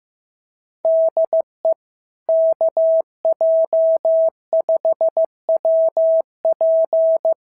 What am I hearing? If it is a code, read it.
Morse code, heard as a keyed tone, about 14 wpm, 650 Hz. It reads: DE KJ5WP
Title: SOTA QSO Practice 15wpm